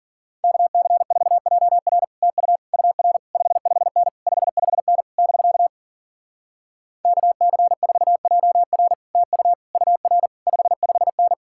Morse code, heard as a keyed tone, about 52 wpm, 700 Hz. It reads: KC4JR TU UR 55N 55N <BK> KC4JR TU UR 55N